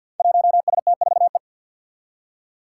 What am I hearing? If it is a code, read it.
Morse code, heard as a keyed tone, about 49 wpm, 700 Hz. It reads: 1ST4E
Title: Five Challenging Letter-Number Combinations 50wpm